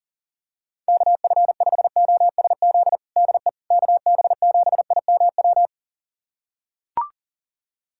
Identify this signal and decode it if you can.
Morse code, heard as a keyed tone, about 40 wpm, 700 Hz. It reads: KF5OSZ DE KB7IMW
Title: USA Callsign DE Callsign 40wpm